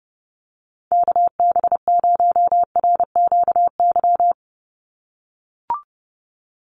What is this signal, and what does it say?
Morse code, heard as a keyed tone, about 30 wpm, 700 Hz. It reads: KB0RQY
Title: USA Call Signs 30wpm